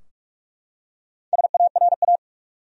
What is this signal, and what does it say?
Morse code, heard as a keyed tone, about 46 wpm, 700 Hz. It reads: SARA
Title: USA Common Names - 45wpm